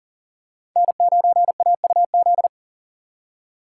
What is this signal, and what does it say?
Morse code, heard as a keyed tone, about 40 wpm, 700 Hz. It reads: N9AUZ
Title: USA Call Signs - 5 Characters - 3x Character Spacing - 40wpm